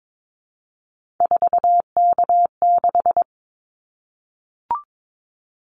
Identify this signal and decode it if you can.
Morse code, heard as a keyed tone, about 22 wpm, 700 Hz. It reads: 4K6
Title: Callsign Prefixes for Top 100 DXCC - 22wpm